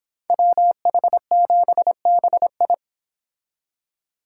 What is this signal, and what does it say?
Morse code, heard as a keyed tone, about 26 wpm, 700 Hz. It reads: WH7BI